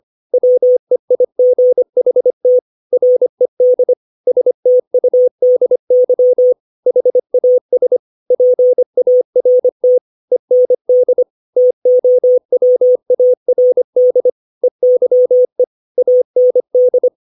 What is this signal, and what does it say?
Morse code, heard as a keyed tone, about 25 wpm, 500 Hz. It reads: WEIGHT RED STUDY HAS PART END TOWARD EYE AND
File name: 632 words / 632 Words 25WPM